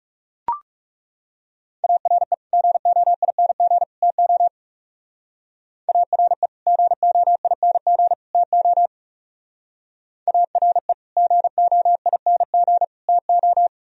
Morse code, heard as a keyed tone, about 36 wpm, 700 Hz. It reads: ARE GOING TO ARE GOING TO ARE GOING TO
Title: Speed-Racing - Sets of 3 Words 30wpm